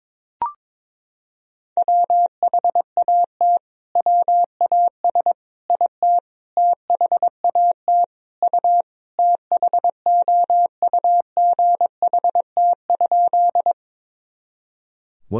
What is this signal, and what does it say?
Morse code, heard as a keyed tone, about 22 wpm, 700 Hz. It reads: WHAT WAS IT THAT U THOUGHT?
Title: Sentences from Top 300 Words and Abbreviations 22wpm